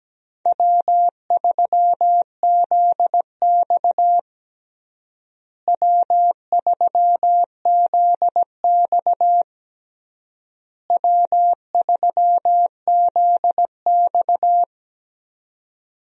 Morse code, heard as a keyed tone, about 17 wpm, 700 Hz. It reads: W3ZX W3ZX W3ZX
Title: USA Call Signs - 4 Characters - Repeated 3x - 17wpm